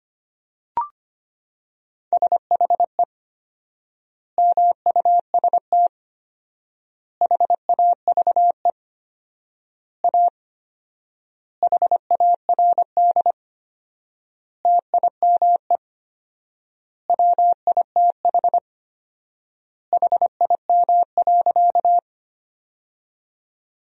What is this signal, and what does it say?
Morse code, heard as a keyed tone, about 25 wpm, 700 Hz. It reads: SHE MUST HAVE A HARD TIME WITH HIM.
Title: 4x Word-Spacing - Sentences from Top 300 Words - 25wpm